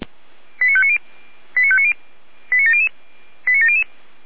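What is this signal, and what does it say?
Five-tone selective call (ZVEI), heard as quick runs of five tones, tones of about 70 ms. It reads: 89590 89590 89700 89700
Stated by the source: Five-tone ZVEI selective call 89280, received by SDR.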